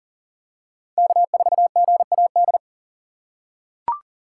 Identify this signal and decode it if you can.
Morse code, heard as a keyed tone, about 40 wpm, 700 Hz. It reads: K4GAD
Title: USA Call Signs - 6x Character Spacing - 40wpm